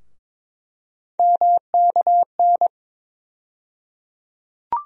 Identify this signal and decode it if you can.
Morse code, heard as a keyed tone, about 22 wpm, 700 Hz. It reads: MKN
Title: USA Call Sign Postfixes - 3 Letters 22wpm